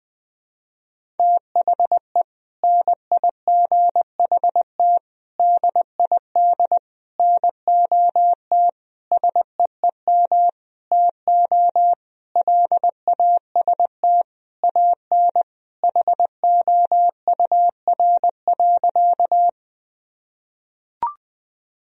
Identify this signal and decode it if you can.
Morse code, heard as a keyed tone, about 20 wpm, 700 Hz. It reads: THE NIGHT DID NOT SEEM TO LAST AN HOUR.
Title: Sentences from Top 1000 Words - Review Part 1 - 20wpm